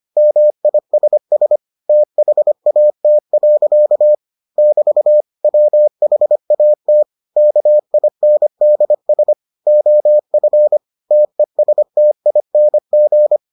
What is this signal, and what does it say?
Morse code, heard as a keyed tone, about 25 wpm, 600 Hz. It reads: MISS THAT. <BT> WHAT KINDS OF TESTING